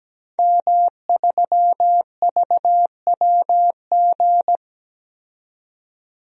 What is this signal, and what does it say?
Morse code, heard as a keyed tone, about 17 wpm, 700 Hz. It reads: M3VWG
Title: DXCC-100 Call Signs - 5 Characters - 17wpm